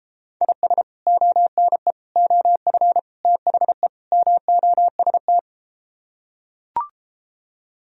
Morse code, heard as a keyed tone, about 33 wpm, 700 Hz. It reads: IS ONE OF THE MOST